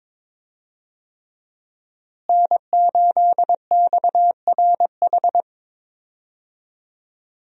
Morse code, heard as a keyed tone, about 21 wpm, 700 Hz. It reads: N8XRH
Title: USA Call Signs - 5 Characters - Repeated 3x - 22wpm